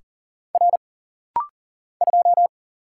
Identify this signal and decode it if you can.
Morse code, heard as a keyed tone, about 40 wpm, 700 Hz. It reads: R 2
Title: Single Letter-Number Alt - Rapid-Fire 40wpm